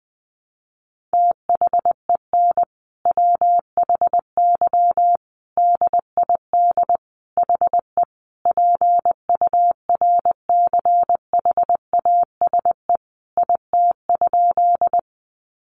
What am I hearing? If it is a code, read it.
Morse code, heard as a keyed tone, about 20 wpm, 700 Hz. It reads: THEN WHY DID HE PURCHASE IT?
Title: Sentences from Top 800 Words 20wpm